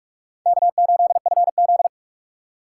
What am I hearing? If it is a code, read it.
Morse code, heard as a keyed tone, about 59 wpm, 700 Hz. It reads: K8FZ